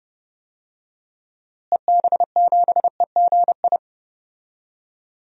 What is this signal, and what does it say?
Morse code, heard as a keyed tone, about 30 wpm, 700 Hz. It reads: EB7EGI
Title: DXCC-100 Call Signs - 6 Characters - Repeated 3x - 30wpm